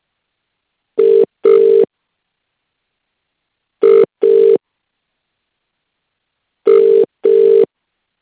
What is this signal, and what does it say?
Ringback tone (UK), 400 + 450 Hz.